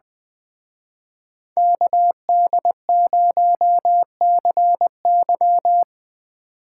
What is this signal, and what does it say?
Morse code, heard as a keyed tone, about 25 wpm, 700 Hz. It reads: KD0CY
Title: Speed Racing - USA 2x2 Call Signs 20wpm